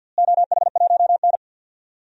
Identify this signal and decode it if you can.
Morse code, heard as a keyed tone, about 47 wpm, 700 Hz. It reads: OH1N
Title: DXCC-100 Call Signs - 4 Characters - Repeated 3x - 50wpm